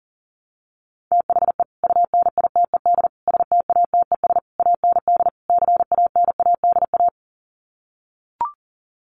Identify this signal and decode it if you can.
Morse code, heard as a keyed tone, about 40 wpm, 700 Hz. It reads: THE UNITED STATES AND CANADA
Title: Sets of 5 Words 40wpm